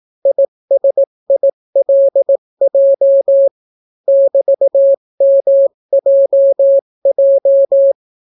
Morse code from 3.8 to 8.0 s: <BT>MJJ